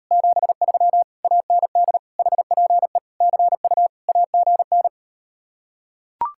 Morse code, heard as a keyed tone, about 38 wpm, 700 Hz. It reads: 73 AND HPE CU AGN